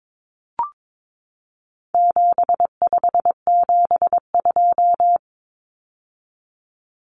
Morse code, heard as a keyed tone, about 22 wpm, 700 Hz. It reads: 7572
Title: Numbers - 4-Digit 22wpm